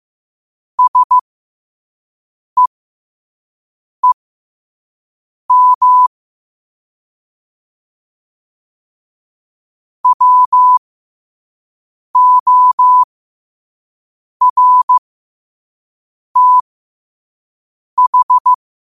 Morse code, heard as a keyed tone, about 15 wpm, spaced as for 5 wpm, 1000 Hz. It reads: SEEM WORTH
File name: qotdfiles / 5wpm